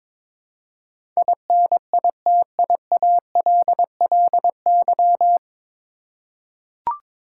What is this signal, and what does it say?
Morse code, heard as a keyed tone, about 22 wpm, 700 Hz. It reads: INITIALLY
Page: Morse Code Ninja practice 22wpm